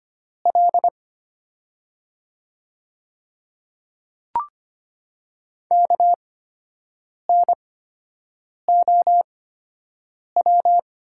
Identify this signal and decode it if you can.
Morse code, heard as a keyed tone, about 25 wpm, spaced as for 7 wpm, 700 Hz. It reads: L KNOW
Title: 8x Character Spacing - Top 100 Words - 25wpm